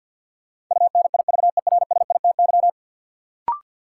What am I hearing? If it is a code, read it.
Morse code, heard as a keyed tone, about 52 wpm, 700 Hz. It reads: UNIVERSITY
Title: Sentences from Top 1000 Words - Review Part 4 - New Word Drill - 50wpm